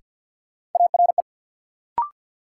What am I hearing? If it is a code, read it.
Morse code, heard as a keyed tone, about 53 wpm, 700 Hz. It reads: ARE
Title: Sentences from Top 100 Words - New Word Drill - 50wpm